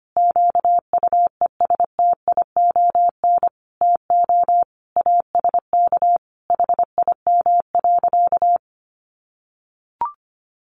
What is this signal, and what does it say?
Morse code, heard as a keyed tone, about 25 wpm, 700 Hz. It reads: QUESTION TO ASK HIM.
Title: Sentences from Top 1000 Words 25wpm